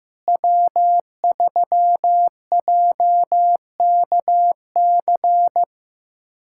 Morse code, heard as a keyed tone, about 15 wpm, 700 Hz. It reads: W3JKC
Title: USA Callsign DE Callsign 15wpm